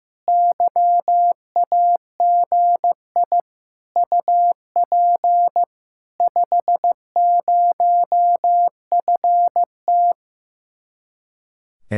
Morse code, heard as a keyed tone, about 15 wpm, 700 Hz. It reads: YAGI UP 50FT
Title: Speed-Racing Common QSO Phrases Encore - 15wpm